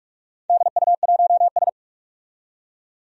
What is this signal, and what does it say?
Morse code, heard as a keyed tone, about 46 wpm, 700 Hz. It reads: DU1S
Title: DXCC-100 Call Signs - 4 Characters - 45wpm